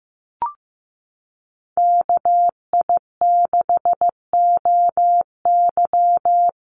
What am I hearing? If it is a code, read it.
Morse code, heard as a keyed tone, about 15 wpm, 700 Hz. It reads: KI6OY
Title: Call Signs Repeated 3x - 15wpm